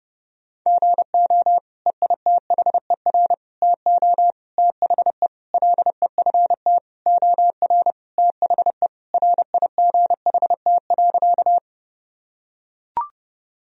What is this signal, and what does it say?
Morse code, heard as a keyed tone, about 30 wpm, 700 Hz. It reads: GO EITHER TO THE LEFT OR THE RIGHT.
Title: Sentences from Top 1000 Words - Review Part 4 - 30wpm